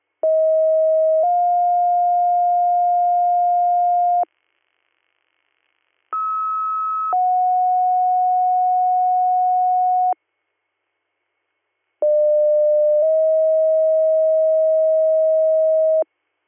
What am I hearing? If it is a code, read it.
Two-tone page, heard as a one-second tone, then a three-second tone, about 640 then 710 Hz, about 1290 then 730 Hz, about 600 then 640 Hz. It is heard from 0.2 to 16.0 s.